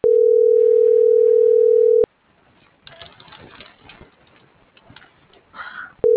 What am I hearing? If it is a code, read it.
Ringback tone (North America), 440 + 480 Hz.
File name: CENTRAL MABE / out maria